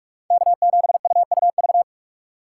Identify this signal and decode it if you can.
Morse code, heard as a keyed tone, about 46 wpm, 700 Hz. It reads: K7UUV
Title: USA Call Signs - 2x Character Spacing - 45wpm